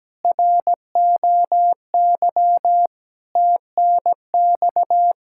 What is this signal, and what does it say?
Morse code, heard as a keyed tone, about 17 wpm, 700 Hz. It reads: ROY TNX
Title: K1USN SST QSO Practice 17wpm